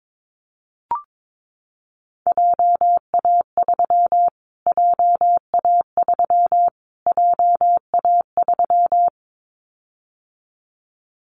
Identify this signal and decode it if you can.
Morse code, heard as a keyed tone, about 22 wpm, 700 Hz. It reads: JA3 JA3 JA3
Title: Callsign Prefixes for Top 25 DXCC - Repeated 3x - 22wpm